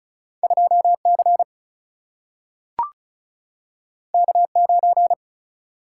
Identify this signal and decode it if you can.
Morse code, heard as a keyed tone, about 36 wpm, 700 Hz. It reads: 2C K9
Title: Single-Letter Single-Number 35wpm